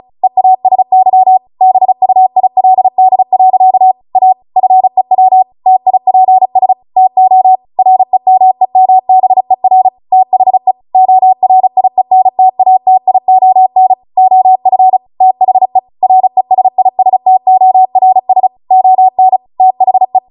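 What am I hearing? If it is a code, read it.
Morse code, heard as a keyed tone, about 36 wpm, 750 Hz. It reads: EASY BUILD. A FEW TIPS TO REMEMBER THE ORIENTATION OF THE RESISTORS ON THE
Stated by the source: Amateur radio Morse code practice archive transmission at 35 wpm.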